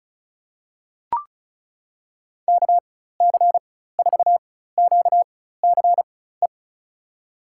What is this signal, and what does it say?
Morse code, heard as a keyed tone, about 36 wpm, spaced as for 16 wpm, 700 Hz. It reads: KC4QCE
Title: USA Call Signs - 6 Characters - 4x Character Spacing - 35wpm